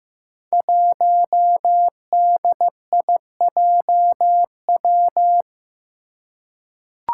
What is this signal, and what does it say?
Morse code, heard as a keyed tone, about 15 wpm, 700 Hz. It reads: 1DIJW
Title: Five Challenging Letter-Number Combinations 15wpm